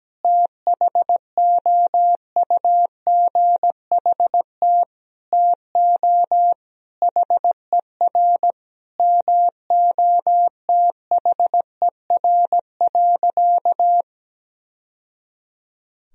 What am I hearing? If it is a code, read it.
Morse code, heard as a keyed tone, about 17 wpm, 700 Hz. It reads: THOUGHT TO HER MOTHER.
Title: Sentences from Top 1000 Words - Review Part 2 - 17wpm